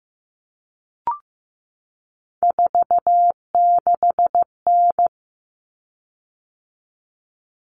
Morse code, heard as a keyed tone, about 15 wpm, 700 Hz. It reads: 46N
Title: Common Cut Numbers - 3-Digit 15wpm